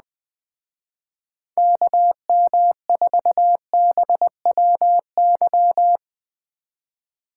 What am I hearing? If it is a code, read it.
Morse code, heard as a keyed tone, about 19 wpm, 700 Hz. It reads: KM4BWY
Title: Speed Racing - USA 2x3 Call Signs 17wpm